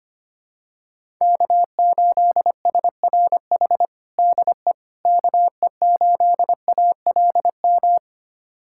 Morse code, heard as a keyed tone, about 25 wpm, 700 Hz. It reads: K8SRH DE KE8ALM
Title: USA Callsign DE Callsign 25wpm